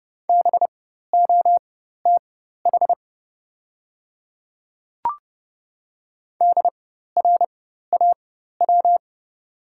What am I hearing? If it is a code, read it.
Morse code, heard as a keyed tone, about 30 wpm, spaced as for 14 wpm, 700 Hz. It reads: BOTH DRAW
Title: 4x Character Spacing - Top 300 Words - 30wpm